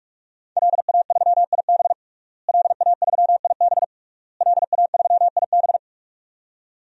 Morse code, heard as a keyed tone, about 46 wpm, 700 Hz. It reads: LA3IB LA3IB LA3IB
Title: DXCC-100 Call Signs - 5 Characters - Repeated 3x - 45wpm